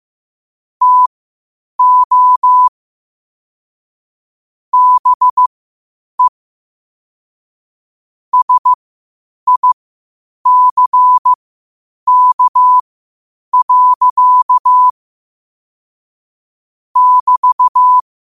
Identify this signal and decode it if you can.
Morse code, heard as a keyed tone, about 15 wpm, spaced as for 8 wpm, 1000 Hz. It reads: TO BE SICK. <BT>
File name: qotdfiles / 8wpm